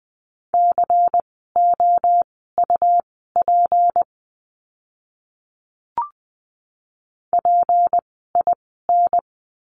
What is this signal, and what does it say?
Morse code, heard as a keyed tone, about 20 wpm, spaced as for 14 wpm, 700 Hz. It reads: COUP PIN
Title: 153 Lesson 25f - Letter p - Words 20wpm